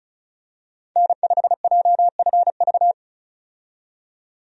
Morse code, heard as a keyed tone, about 34 wpm, 700 Hz. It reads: N5JFV
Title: USA Call Signs - 5 Characters - 35wpm